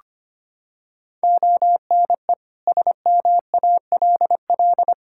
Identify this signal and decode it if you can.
Morse code, heard as a keyed tone, about 25 wpm, 700 Hz. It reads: ONE SMALL
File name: Sets of 9 Words - 25wpm